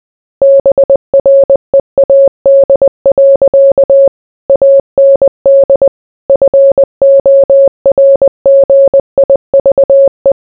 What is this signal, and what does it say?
Morse code, heard as a keyed tone, about 20 wpm, 550 Hz. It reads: BREAD. AND FORGIVE